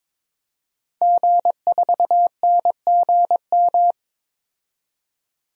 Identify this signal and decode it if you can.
Morse code, heard as a keyed tone, about 22 wpm, 700 Hz. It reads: G4NGM
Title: DXCC-100 Call Signs - 5 Characters - 22wpm